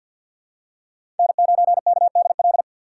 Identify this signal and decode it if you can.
Morse code, heard as a keyed tone, about 49 wpm, 700 Hz. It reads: N9KDL
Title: USA Call Signs - 3x Character Spacing - 50wpm